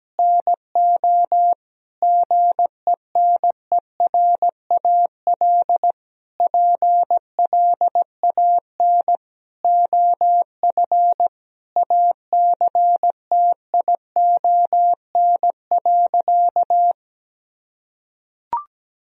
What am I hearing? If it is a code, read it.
Morse code, heard as a keyed tone, about 17 wpm, 700 Hz. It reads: NO GENERAL PLAN OF ACTION.
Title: Sentences from Top 1000 Words - Review Part 4 - 17wpm